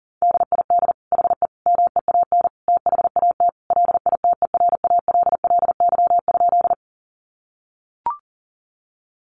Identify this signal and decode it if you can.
Morse code, heard as a keyed tone, about 37 wpm, 700 Hz. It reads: DID HE MEAN THAT LITERALLY?
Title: Sentences from Top 1400 Words 40wpm